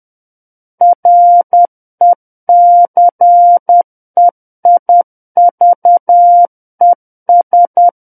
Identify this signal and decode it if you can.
Morse code, heard as a keyed tone, about 10 wpm, 700 Hz. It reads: RECEIVES
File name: Book One / 01-Chapter 1 - 10 WPM